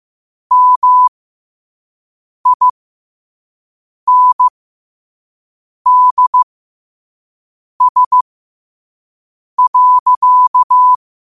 Morse code, heard as a keyed tone, about 15 wpm, spaced as for 5 wpm, 1000 Hz. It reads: MINDS.
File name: qotdfiles / 5wpm